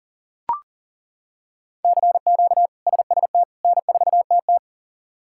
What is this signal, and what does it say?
Morse code, heard as a keyed tone, about 40 wpm, 700 Hz. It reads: CQ SST N4TT
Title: K1USN SST QSO Practice - 40wpm